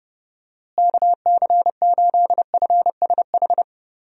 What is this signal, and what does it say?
Morse code, heard as a keyed tone, about 30 wpm, 700 Hz. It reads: KC8FSH
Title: USA Call Signs - 6 Characters - Repeated 2x - 30wpm